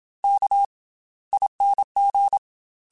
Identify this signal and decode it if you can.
Morse code, heard as a keyed tone, about 27 wpm, 800 Hz. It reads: K ING
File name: Special INGs / Special INGs-mp3s / Slow ing @25